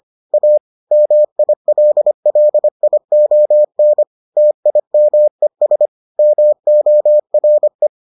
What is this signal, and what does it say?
Morse code, heard as a keyed tone, about 25 wpm, 600 Hz. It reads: A MILLION TIMES MORE